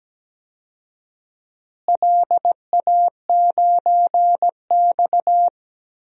Morse code, heard as a keyed tone, about 17 wpm, 700 Hz. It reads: LA9X
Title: DXCC-100 Call Signs - 4 Characters - Repeated 2x - 17wpm